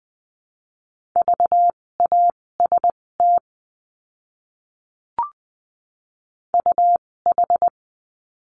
Morse code, heard as a keyed tone, about 20 wpm, spaced as for 16 wpm, 700 Hz. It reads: VAST UH
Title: 256 Lesson 42a - Words 12-20wpm